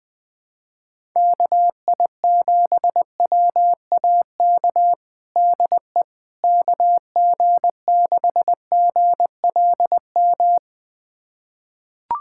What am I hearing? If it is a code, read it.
Morse code, heard as a keyed tone, about 20 wpm, 700 Hz. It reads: KI7WAK DE KG6GLM
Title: USA Callsign DE Callsign - 6 Characters - 20wpm